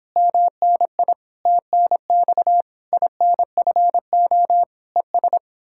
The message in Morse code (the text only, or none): MNI TNX INFO ES